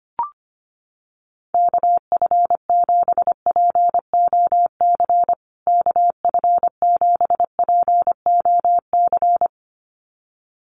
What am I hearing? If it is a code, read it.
Morse code, heard as a keyed tone, about 25 wpm, 700 Hz. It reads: KF7POC KF7POC
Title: USA Call Signs Repeated 2x 25wpm